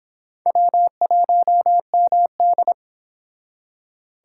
Morse code, heard as a keyed tone, about 25 wpm, 700 Hz. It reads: W1MD